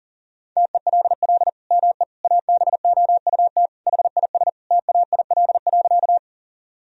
Morse code, heard as a keyed tone, about 40 wpm, 700 Hz. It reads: TELL ME ABOUT HIS TAIL.
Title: Sentences from Top 1000 Words - Review Part 3 - 40wpm